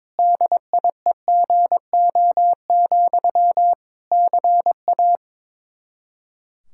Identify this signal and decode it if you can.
Morse code, heard as a keyed tone, about 22 wpm, 700 Hz. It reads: DIEGO, CA
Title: Practice QSO - Exchange 1 - Beginning - 22wpm